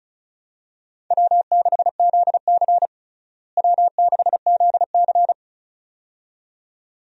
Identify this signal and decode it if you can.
Morse code, heard as a keyed tone, about 36 wpm, 700 Hz. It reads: W6ZC W6ZC
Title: USA Call Signs - 4 Characters - Repeated 2x - 35wpm